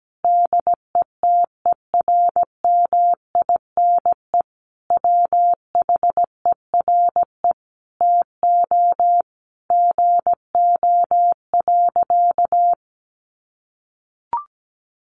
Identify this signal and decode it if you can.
Morse code, heard as a keyed tone, about 17 wpm, 700 Hz. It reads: DETERMINE WHERE TO GO.
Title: Sentences from Top 700 Words 17wpm